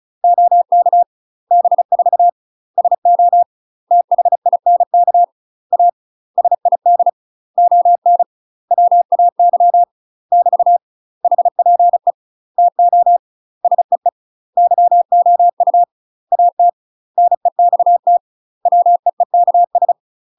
Morse code, heard as a keyed tone, about 34 wpm, 700 Hz. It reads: OK B4 SO THINK A SID ON WAY <BT> HPE TO SEE YOU AT NEXT WEEKS
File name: UK MASTER QSOs TRAINING / 35WPM / UK MASTER QSOs 0001